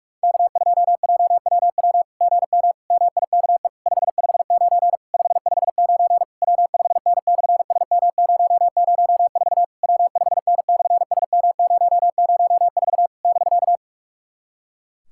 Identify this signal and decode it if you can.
Morse code, heard as a keyed tone, about 46 wpm, 700 Hz. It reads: K2JWW GM MIKE 559 559 W5N/SM004 W5N/SM004 <BK>
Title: SOTA QSO Practice 45wpm